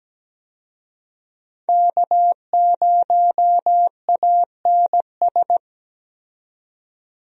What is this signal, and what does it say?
Morse code, heard as a keyed tone, about 17 wpm, 700 Hz. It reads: K0ANS
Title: USA Call Signs - 5 Characters - 17wpm